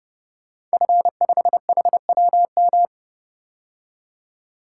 Morse code, heard as a keyed tone, about 30 wpm, 700 Hz. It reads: F5HWM
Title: Code Groups - 5 Letter:Numbers - 30wpm